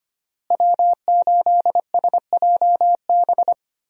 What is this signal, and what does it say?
Morse code, heard as a keyed tone, about 25 wpm, 700 Hz. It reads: W8SJB
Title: USA Call Signs - 5 Characters - Repeated 3x - 25wpm